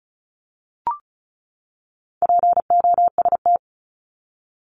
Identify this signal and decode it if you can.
Morse code, heard as a keyed tone, about 34 wpm, 700 Hz. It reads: POST
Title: Top 751-1000 Words 35wpm